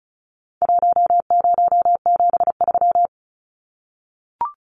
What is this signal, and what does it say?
Morse code, heard as a keyed tone, about 34 wpm, 700 Hz. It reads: 1073
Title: Numbers - 4-Digit 35wpm